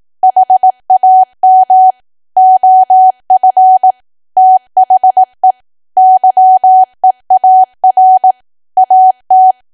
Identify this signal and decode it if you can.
Morse code, heard as a keyed tone, about 18 wpm, 750 Hz. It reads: HAM OF THE YEAR AT
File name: Archive / 18 WPM / 18WPM